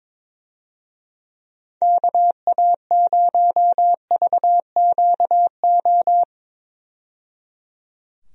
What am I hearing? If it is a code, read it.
Morse code, heard as a keyed tone, about 21 wpm, 700 Hz. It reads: KA0VQO
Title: USA Call Signs - 6 Characters - 22wpm